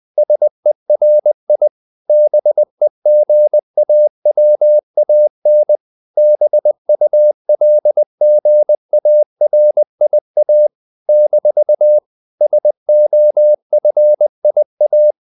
Morse code, heard as a keyed tone, about 20 wpm, 600 Hz. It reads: SERI BEGAWAN BULGARIA - SOFIA
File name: CW / Practice / NationalCapitals20